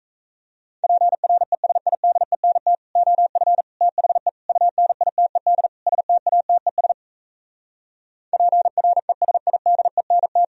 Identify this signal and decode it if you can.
Morse code, heard as a keyed tone, about 37 wpm, 700 Hz. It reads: PRESIDENT OF THE UNITED STATES PRESIDENT